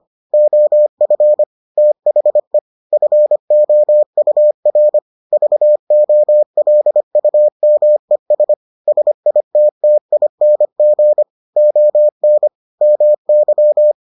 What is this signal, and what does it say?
Morse code, heard as a keyed tone, about 25 wpm, 600 Hz. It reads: OF THE FOUR VOLUMES SITTING ON MY